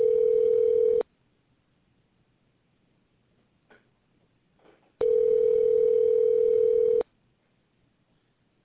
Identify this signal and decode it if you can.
Ringback tone (North America), 440 + 480 Hz.